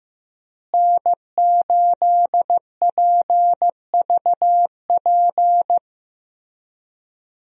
Morse code, heard as a keyed tone, about 15 wpm, 700 Hz. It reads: N8PVP
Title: USA Call Signs - 5 Characters - Repeated 3x - 15wpm